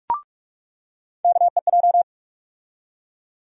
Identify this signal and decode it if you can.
Morse code, heard as a keyed tone, about 46 wpm, 700 Hz. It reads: KEJ
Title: USA Call Sign Postfixes - 3 Letters 45wpm